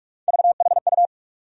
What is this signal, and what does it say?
Morse code, heard as a keyed tone, about 46 wpm, 700 Hz. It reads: VHU
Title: USA Call Sign Postfixes - 3 Letters 45wpm